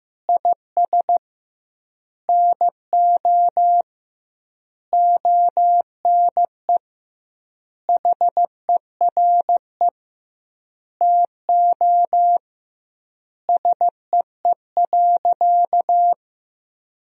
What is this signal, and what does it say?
Morse code, heard as a keyed tone, about 16 wpm, 700 Hz. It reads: IS NO ONE HERE TO SEE.
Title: 2x Word-Spacing - Sentences from Top 200 Words - 15wpm